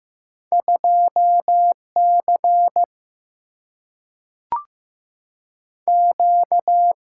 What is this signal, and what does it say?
Morse code, heard as a keyed tone, about 15 wpm, 700 Hz. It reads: 2 C Q
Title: Single-Letter Single-Number 15wpm